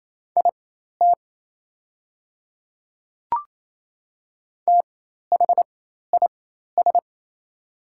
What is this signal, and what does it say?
Morse code, heard as a keyed tone, about 28 wpm, spaced as for 13 wpm, 700 Hz. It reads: IT THIS
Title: 4x Character Spacing - Top 100 Words - 28wpm